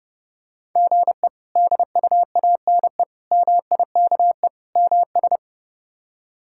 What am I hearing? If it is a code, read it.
Morse code, heard as a keyed tone, about 30 wpm, 700 Hz. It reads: GE DUANE MIKE MS
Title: K1USN SST QSO Practice - 30wpm